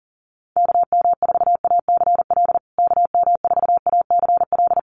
Morse code, heard as a keyed tone, about 40 wpm, 700 Hz. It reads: KM4ACL KM4ACL
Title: USA Call Signs Repeated 3x 40wpm